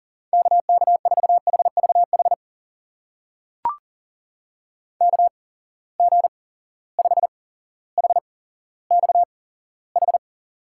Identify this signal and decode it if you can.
Morse code, heard as a keyed tone, about 39 wpm, spaced as for 17 wpm, 700 Hz. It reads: KK4HVH KG5HXH
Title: USA Call Signs - 6 Characters - 8x Character Spacing - 40wpm